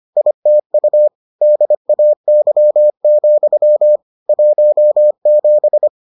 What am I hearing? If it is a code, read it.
Morse code, heard as a keyed tone, about 25 wpm, 600 Hz. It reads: ITU DAY, 17